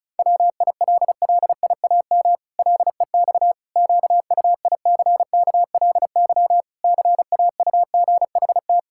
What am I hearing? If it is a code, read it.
Morse code, heard as a keyed tone, about 34 wpm, 700 Hz. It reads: WILLIAM LEX QUICKLY CAUGHT
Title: Pangram Challenge v2 - 35wpm